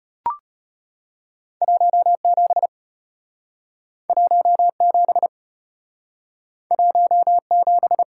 Morse code, heard as a keyed tone, about 33 wpm, 700 Hz. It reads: 17 17 17